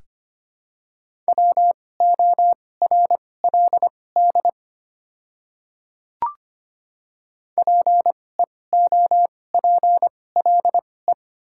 Morse code, heard as a keyed tone, about 25 wpm, spaced as for 18 wpm, 690 Hz. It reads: WORLD PEOPLE
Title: 2x Character Spacing - Top 200 Words - 25wpm